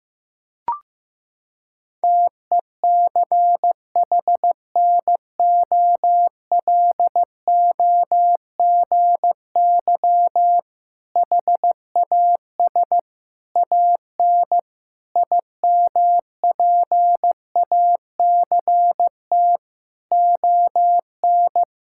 Morse code, heard as a keyed tone, about 15 wpm, 700 Hz. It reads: TECHNOLOGY HAS AN IMPACT ON
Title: Sentences from Top 1000 Words - Review Part 6 - 15wpm